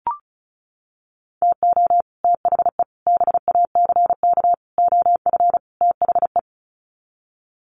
Morse code, heard as a keyed tone, about 34 wpm, 700 Hz. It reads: TO THE BACK OF THE
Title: Sets of 5 Words 35wpm